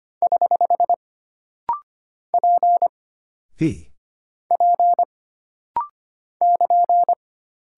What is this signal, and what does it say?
Morse code, heard as a keyed tone, about 25 wpm, 700 Hz. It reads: <HH> P P <KN>
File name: Single Character (Letter:Number:Punctuation:Prosign) - Rapid-Fire - 25wpm